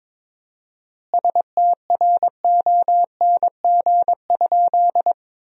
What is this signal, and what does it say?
Morse code, heard as a keyed tone, about 22 wpm, 700 Hz. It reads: STRONG?
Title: 4x Word-Spacing - Sentences from Top 500 Words - 22wpm